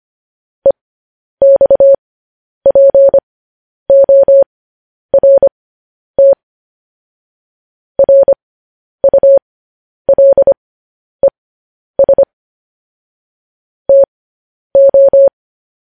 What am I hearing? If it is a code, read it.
Morse code, heard as a keyed tone, about 25 wpm, spaced as for 10 wpm, 550 Hz. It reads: EXPORT RULES TO
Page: news-headlines-in-morse-code-at-10-wpm